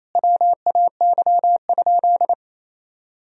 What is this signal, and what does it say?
Morse code, heard as a keyed tone, about 28 wpm, 700 Hz. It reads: WAY?
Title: Sentences from Top 200 Words and Abbreviations 28wpm